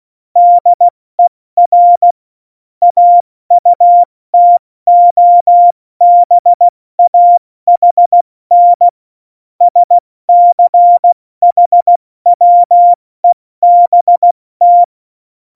Morse code, heard as a keyed tone, about 16 wpm, 700 Hz. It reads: DER AUTOBAHN SCHWEBT